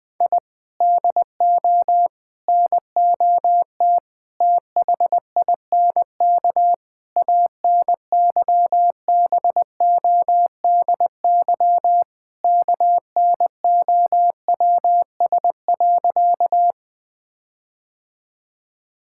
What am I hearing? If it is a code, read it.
Morse code, heard as a keyed tone, about 21 wpm, 700 Hz. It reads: I DO NOT THINK ANYBODY KNOWS.
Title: Sentences with 5-Word Phrases - 20wpm